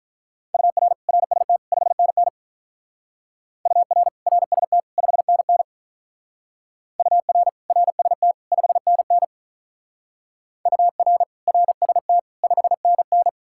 Morse code, heard as a keyed tone, about 45 wpm, 700 Hz. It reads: UR RST 5NN UR RST 5NN UR RST 5NN UR RST 5NN